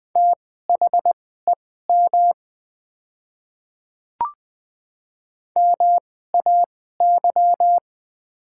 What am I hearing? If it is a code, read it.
Morse code, heard as a keyed tone, about 20 wpm, spaced as for 14 wpm, 700 Hz. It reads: THEM MAY
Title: 2x Character Spacing - Top 100 Words - 20wpm